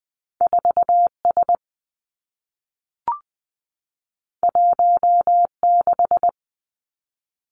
Morse code, heard as a keyed tone, about 20 wpm, 700 Hz. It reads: 4 S 1 6
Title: Two Challenging Letter-Number Combinations 20wpm